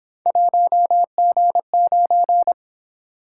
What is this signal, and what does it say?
Morse code, heard as a keyed tone, about 27 wpm, 700 Hz. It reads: 1G9
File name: Speed-Racing - Three Letter-Number Combinations - 22wpm